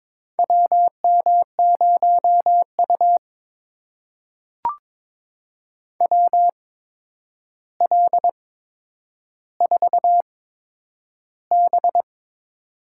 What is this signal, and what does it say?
Morse code, heard as a keyed tone, about 23 wpm, spaced as for 10 wpm, 700 Hz. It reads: WM0U WL4B
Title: USA Call Signs - 4 Characters - 8x Character Spacing - 22wpm